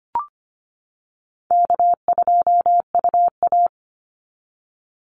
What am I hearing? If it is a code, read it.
Morse code, heard as a keyed tone, about 25 wpm, 700 Hz. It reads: K2UA
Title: Easy Call Signs - 25wpm